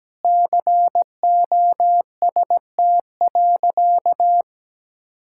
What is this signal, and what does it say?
Morse code, heard as a keyed tone, about 17 wpm, 700 Hz. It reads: COST.
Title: Sentences from Top 1000 Words - Review Part 5 - 17wpm